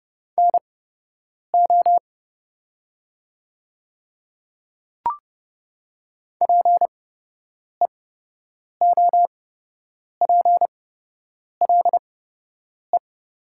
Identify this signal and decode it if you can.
Morse code, heard as a keyed tone, about 32 wpm, spaced as for 8 wpm, 700 Hz. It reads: NO PEOPLE
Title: 8x Character Spacing - Top 100 Words - 30wpm